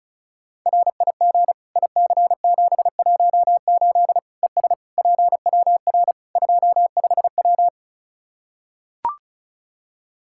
Morse code, heard as a keyed tone, about 36 wpm, 700 Hz. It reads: RIG IC718 ES PWR 25W
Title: Practice QSO - Exchange 2 - Info - 35wpm